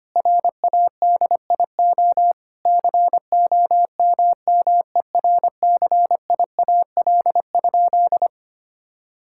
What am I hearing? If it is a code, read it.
Morse code, heard as a keyed tone, about 25 wpm, 700 Hz. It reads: RADIO COMMERCIAL?
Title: Sentences from Top 1000 Words - Review Part 5 - 25wpm